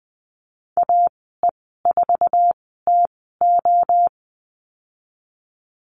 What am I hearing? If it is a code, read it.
Morse code, heard as a keyed tone, about 19 wpm, spaced as for 14 wpm, 700 Hz. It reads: AE4TO
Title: 055 Lesson 10g - Letter r - Call Signs 20wpm